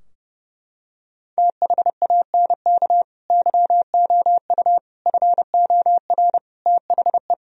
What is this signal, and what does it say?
Morse code, heard as a keyed tone, about 30 wpm, 700 Hz. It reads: THANK YOU FOR THE
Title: Sentences from Top 1000 Words - Review Part 3 - 30wpm